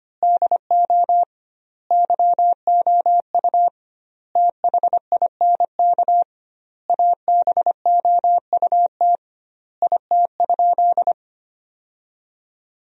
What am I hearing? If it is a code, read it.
Morse code, heard as a keyed tone, about 25 wpm, 700 Hz. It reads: DO YOU THINK ABOUT IT?
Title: 2x Word-Spacing - Sentences from Top 200 Words - 25wpm